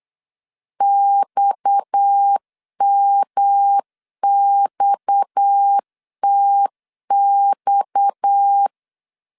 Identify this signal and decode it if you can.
Morse code, heard as a keyed tone, about 8 wpm, 800 Hz. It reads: XMXTX
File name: LEZIONI / 6 PKX0 / 6 PKX 8wpm 3041